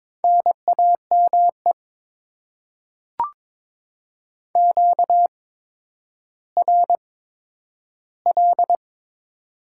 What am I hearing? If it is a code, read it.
Morse code, heard as a keyed tone, about 21 wpm, spaced as for 11 wpm, 700 Hz. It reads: NAME QRL
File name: Top 20 QSO Elements - 8x Character Spacing - 22wpm